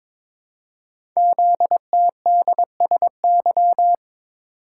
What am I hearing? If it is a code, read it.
Morse code, heard as a keyed tone, about 23 wpm, 700 Hz. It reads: ZTDSY
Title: Code Groups - 5 Letters - 22wpm